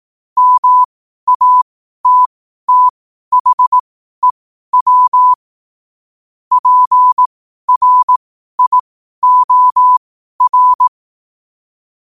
Morse code, heard as a keyed tone, about 18 wpm, spaced as for 12 wpm, 1000 Hz. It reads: MATTHEW PRIOR
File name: qotdfiles / 12wpm